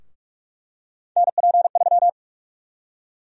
Morse code, heard as a keyed tone, about 46 wpm, 700 Hz. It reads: NP3
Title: USA Call Sign Prefixes - 45wpm